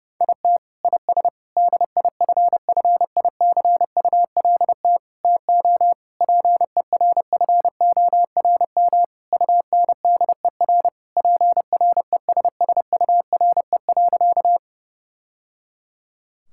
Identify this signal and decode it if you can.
Morse code, heard as a keyed tone, about 30 wpm, 700 Hz. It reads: IT IS DIFFICULT TO PERFORM UNDER PRESSURE.
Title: Sentences from Top 1000 Words - Review Part 6 - 30wpm